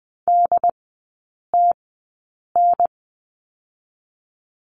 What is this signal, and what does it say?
Morse code, heard as a keyed tone, about 20 wpm, spaced as for 8 wpm, 700 Hz. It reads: DTN
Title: 093 Lesson 16e - Letter c - Three Characters 20wpm